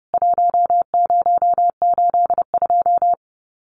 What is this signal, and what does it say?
Morse code, heard as a keyed tone, about 30 wpm, 700 Hz. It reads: 1082
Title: CWT Exchange 30wpm